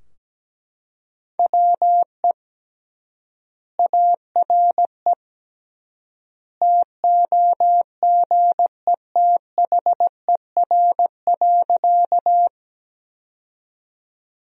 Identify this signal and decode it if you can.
Morse code, heard as a keyed tone, about 17 wpm, 700 Hz. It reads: WE ARE TOGETHER.
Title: 3x Word-Spacing - Sentences from Top 300 Words - 17wpm